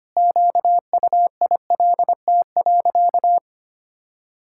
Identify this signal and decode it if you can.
Morse code, heard as a keyed tone, about 25 wpm, 700 Hz. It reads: QUILT.
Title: Pangram Challenge v2 - 25wpm